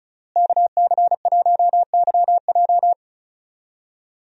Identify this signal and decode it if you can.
Morse code, heard as a keyed tone, about 36 wpm, 700 Hz. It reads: KC1YJ
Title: USA Call Signs - 5 Characters - Repeated 3x - 35wpm